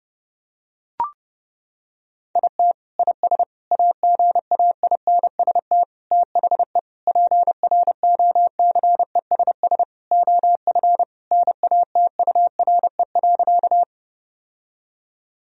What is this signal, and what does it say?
Morse code, heard as a keyed tone, about 30 wpm, 700 Hz. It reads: IT IS AGAINST THE PROCESS OF NATURE.
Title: Sentences from Top 1000 Words - Review Part 3 - 30wpm